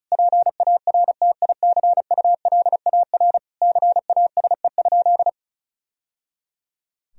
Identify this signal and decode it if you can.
Morse code, heard as a keyed tone, about 36 wpm, 700 Hz. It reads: PARTICULAR CASE?
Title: Sentences from Top 1000 Words - Review Part 4 - 35wpm